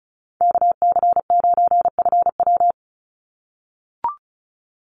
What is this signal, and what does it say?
Morse code, heard as a keyed tone, about 34 wpm, 700 Hz. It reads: KC9FW
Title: USA Call Signs 35wpm